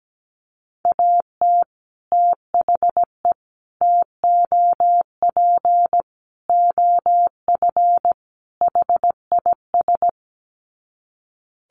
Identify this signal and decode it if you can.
Morse code, heard as a keyed tone, about 17 wpm, 700 Hz. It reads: AT THE TOP OF HIS
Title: Sets of 5 Words Encore 17wpm